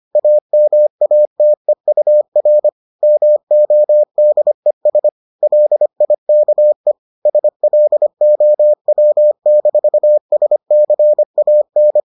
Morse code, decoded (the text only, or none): AMATEUR MODES LIKE SLOW-SCAN